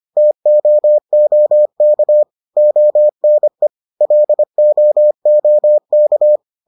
Morse code, heard as a keyed tone, about 25 wpm, 600 Hz. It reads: TOOK ONE LOOK